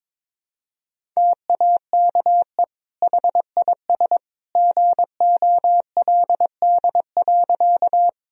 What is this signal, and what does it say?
Morse code, heard as a keyed tone, about 21 wpm, 700 Hz. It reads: TAKE HIS GOLD.
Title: Sentences from Top 1000 Words - Review Part 3 - 22wpm